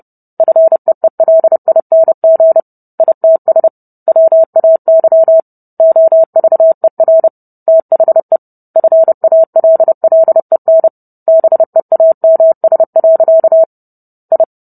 Morse code, heard as a keyed tone, about 30 wpm, 650 Hz. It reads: FEELING ITS WAY OVER THE FALLEN BEAMS. I